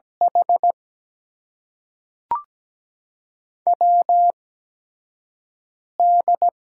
Morse code, heard as a keyed tone, about 17 wpm, spaced as for 4 wpm, 700 Hz. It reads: HWD
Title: USA Call Signs - 6 Characters - 8x Character Spacing - 17wpm